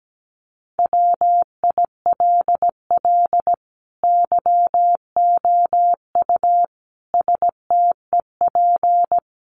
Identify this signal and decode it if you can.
Morse code, heard as a keyed tone, about 17 wpm, 700 Hz. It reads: WILL YOU STEP
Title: Sentences from Top 1300 Words 17wpm